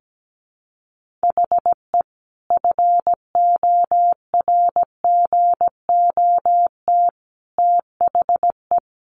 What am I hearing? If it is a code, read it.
Morse code, heard as a keyed tone, about 17 wpm, 700 Hz. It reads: HE FORGOT THE
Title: Sentences from Top 1400 Words 17wpm